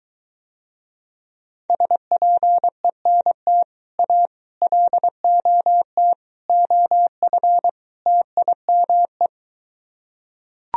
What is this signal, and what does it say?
Morse code, heard as a keyed tone, about 23 wpm, 700 Hz. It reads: SPENT A LOT OF TIME